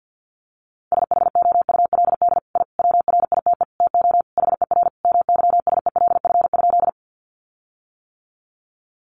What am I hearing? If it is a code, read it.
Morse code, heard as a keyed tone, about 50 wpm, 700 Hz. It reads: SHOULD I WRITE TO HER MYSELF?
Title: Sentences from Top 1300 Words 50wpm